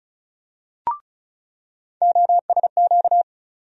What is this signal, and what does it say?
Morse code, heard as a keyed tone, about 34 wpm, 700 Hz. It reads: OSQ
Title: USA Call Sign Postfixes - 3 Letters 35wpm